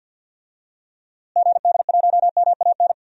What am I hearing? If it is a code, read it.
Morse code, heard as a keyed tone, about 48 wpm, 700 Hz. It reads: GD1MAN
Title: DXCC-100 Call Signs - 6 Characters - Repeated 3x - 50wpm